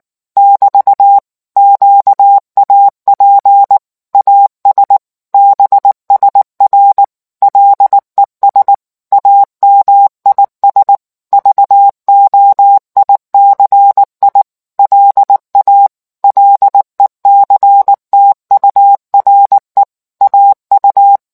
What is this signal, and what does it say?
Morse code, heard as a keyed tone, about 19 wpm, 800 Hz. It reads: <BT> QAP AS BSR LES AMIS VOICI LA LECTURE AU